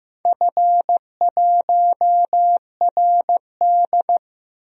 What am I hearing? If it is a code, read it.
Morse code, heard as a keyed tone, about 15 wpm, 700 Hz. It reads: F1RD
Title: DXCC-100 Call Signs - 4 Characters - Repeated 2x - 15wpm